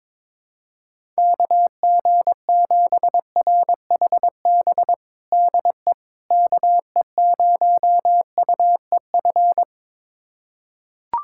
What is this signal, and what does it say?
Morse code, heard as a keyed tone, about 22 wpm, 700 Hz. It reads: KG7RHB DE KE0UEF
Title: USA Callsign DE Callsign - 6 Characters - 22wpm